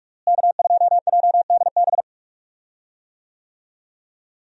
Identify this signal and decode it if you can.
Morse code, heard as a keyed tone, about 47 wpm, 700 Hz. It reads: K2JDB
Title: USA Call Signs - 5 Characters - Repeated 3x - 45wpm